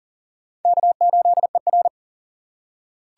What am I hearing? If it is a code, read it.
Morse code, heard as a keyed tone, about 40 wpm, 700 Hz. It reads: K8ER
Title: USA Call Signs - 4 Characters - 3x Character Spacing - 40wpm